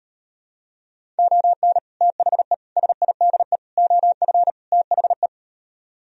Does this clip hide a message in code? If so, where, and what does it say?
Yes: Morse code from 1.2 to 5.3 s, heard as a keyed tone, about 38 wpm, 700 Hz. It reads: ON THE SIDE OF THE